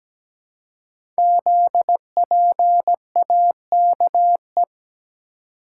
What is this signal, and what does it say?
Morse code, heard as a keyed tone, about 17 wpm, 690 Hz. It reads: ZPAKE
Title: Code Groups - 5 Letter:Numbers - 17wpm